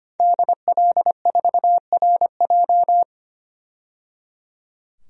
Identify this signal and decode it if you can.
Morse code, heard as a keyed tone, about 25 wpm, 700 Hz. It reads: DL4RJ
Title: DXCC-100 Call Signs - 5 Characters - 25wpm